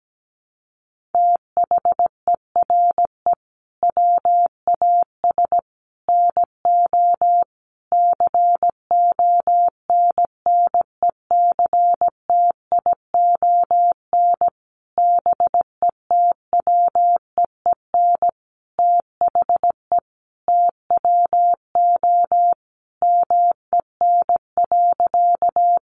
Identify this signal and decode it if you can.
Morse code, heard as a keyed tone, about 17 wpm, 700 Hz. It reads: THERE WAS NO CONNECTION BETWEEN THE TWO MEN.
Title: Sentences from Top 1300 Words 17wpm